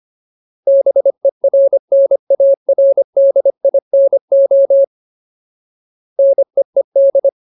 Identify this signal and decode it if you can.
Morse code, heard as a keyed tone, about 25 wpm, 550 Hz. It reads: BERNARDINO NEED
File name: Ultimate Word List / 25WPM Ultimate